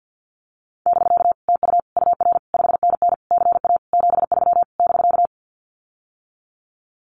Morse code, heard as a keyed tone, about 48 wpm, 700 Hz. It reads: <BK> TU UR 5NN CA 73 <BK>
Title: Common QSO Phrases Encore 50wpm